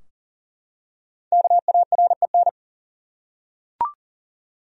Morse code, heard as a keyed tone, about 40 wpm, 700 Hz. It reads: KAREN
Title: USA Common Names - Max 5 Letters - 40wpm